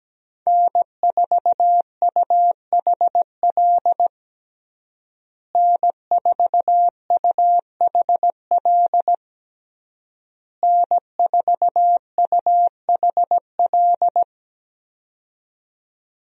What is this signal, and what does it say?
Morse code, heard as a keyed tone, about 17 wpm, 700 Hz. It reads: N4UHL N4UHL N4UHL
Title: USA Call Signs - 5 Characters - Repeated 3x - 17wpm